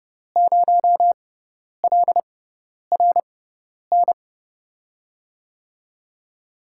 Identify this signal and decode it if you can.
Morse code, heard as a keyed tone, about 31 wpm, spaced as for 10 wpm, 700 Hz. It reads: 0LRN
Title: USA Call Signs - 5 Characters - 6x Character Spacing - 30wpm